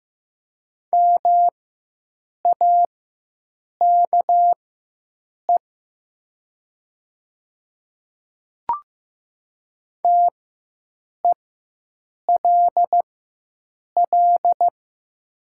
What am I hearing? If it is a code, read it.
Morse code, heard as a keyed tone, about 15 wpm, spaced as for 7 wpm, 700 Hz. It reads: MAKE TELL
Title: 4x Character Spacing - Top 500 Words - 15wpm